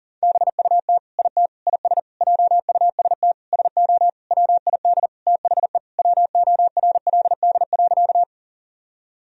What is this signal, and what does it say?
Morse code, heard as a keyed tone, about 40 wpm, 700 Hz. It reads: BUT IT IS JUST SO WID THE WORLD.
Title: Sentences from Top 200 Words and Abbreviations 40wpm